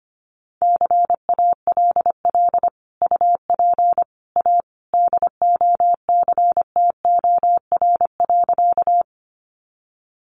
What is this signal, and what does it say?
Morse code, heard as a keyed tone, about 25 wpm, 700 Hz. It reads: CALL UP A DOCTOR.
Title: Sentences from Top 1500 Words 25wpm